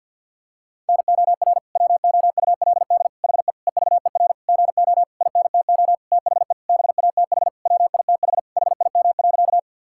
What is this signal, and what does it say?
Morse code, heard as a keyed tone, about 50 wpm, 700 Hz. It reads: NOR WOULD HE EVER GO INTO THE BATH WITH HIM.
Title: Sentences from Top 1600 Words 50wpm